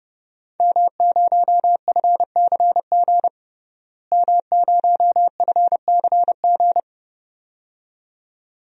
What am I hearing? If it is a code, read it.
Morse code, heard as a keyed tone, about 30 wpm, 700 Hz. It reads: M0FCG M0FCG
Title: DXCC-100 Call Signs - 5 Characters - Repeated 2x - 30wpm